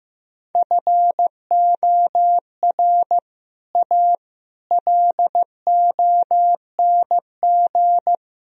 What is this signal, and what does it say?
Morse code, heard as a keyed tone, about 16 wpm, 700 Hz. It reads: FOR A LONG
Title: Speed-Racing - Sets of 3 Words 15wpm